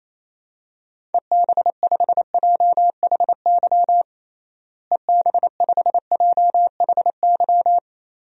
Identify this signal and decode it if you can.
Morse code, heard as a keyed tone, about 28 wpm, 700 Hz. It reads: EB5JHY EB5JHY
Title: DXCC-100 Call Signs - 6 Characters - Repeated 3x - 28wpm